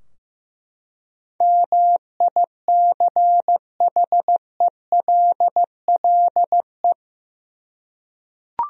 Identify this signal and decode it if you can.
Morse code, heard as a keyed tone, about 15 wpm, 700 Hz. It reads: MICHELLE
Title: USA Common Names - 15wpm